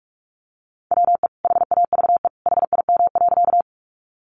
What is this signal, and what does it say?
Morse code, heard as a keyed tone, about 44 wpm, 700 Hz. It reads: WE HAVE HIM.
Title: Sentences from Top 100 Words 45wpm